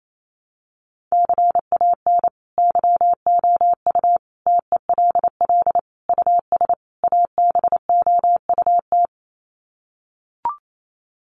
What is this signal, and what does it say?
Morse code, heard as a keyed tone, about 28 wpm, 700 Hz. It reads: CAN YOU TELL US ABOUT
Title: Sets of 5 Words Encore2 28wpm